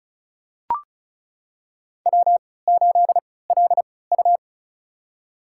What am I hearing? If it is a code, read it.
Morse code, heard as a keyed tone, about 36 wpm, spaced as for 20 wpm, 700 Hz. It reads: W8LU
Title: USA Call Signs - 4 Characters - 3x Character Spacing - 35wpm